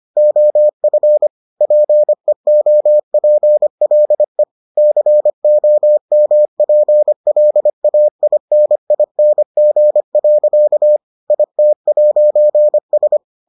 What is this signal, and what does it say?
Morse code, heard as a keyed tone, about 25 wpm, 600 Hz. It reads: OF PEOPLE COMPLAINING. IT'S